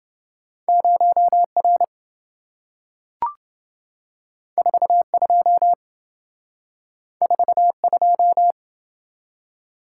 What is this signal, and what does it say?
Morse code, heard as a keyed tone, about 27 wpm, 700 Hz. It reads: 0R 42 42